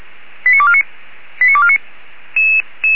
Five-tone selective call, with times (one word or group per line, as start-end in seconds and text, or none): 0.5-0.8 89249
1.4-1.8 89249